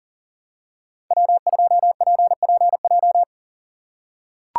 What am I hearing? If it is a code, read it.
Morse code, heard as a keyed tone, about 40 wpm, 700 Hz. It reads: W2PPJ
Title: USA Call Signs - 5 Characters - 3x Character Spacing - 40wpm